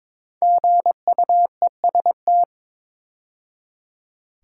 Morse code, heard as a keyed tone, about 22 wpm, 700 Hz. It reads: GUEST
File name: 5 Letter Words - First 3 Letters Given - 22wpm